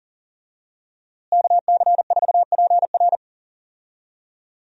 Morse code, heard as a keyed tone, about 37 wpm, 700 Hz. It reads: KC4PR
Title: USA Call Signs - 5 Characters - Repeated 2x - 40wpm